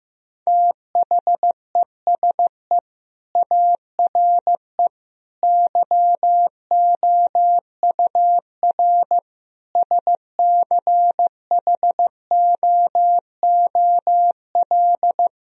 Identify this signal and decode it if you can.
Morse code, heard as a keyed tone, about 15 wpm, 700 Hz. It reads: THESE ARE YOUR SCHOOL
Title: Sentences from Top 1000 Words - Review Part 1 - 15wpm